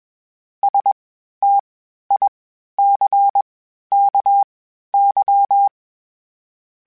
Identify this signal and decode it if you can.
Morse code, heard as a keyed tone, about 21 wpm, spaced as for 12 wpm, 800 Hz. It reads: STICKY
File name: Misc / Misc-mp3s / WordPairs sp @20